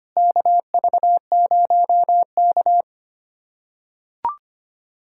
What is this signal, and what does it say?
Morse code, heard as a keyed tone, about 25 wpm, 700 Hz. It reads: KV0K
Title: USA Callsign DE Callsign - 4 Characters - 25wpm